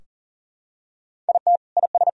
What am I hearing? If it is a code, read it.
Morse code, heard as a keyed tone, about 37 wpm, 700 Hz. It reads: IT IS
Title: Sentences from Top 1000 Words - Review Part 3 - 40wpm